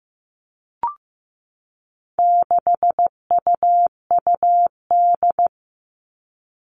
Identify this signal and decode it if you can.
Morse code, heard as a keyed tone, about 15 wpm, 700 Hz. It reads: 6UUD
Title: All Cut Numbers - 4-Digit 15wpm